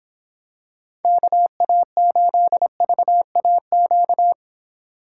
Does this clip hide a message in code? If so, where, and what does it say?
Yes: Morse code from 1.0 to 4.3 s, heard as a keyed tone, about 26 wpm, 700 Hz. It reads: KA8VAQ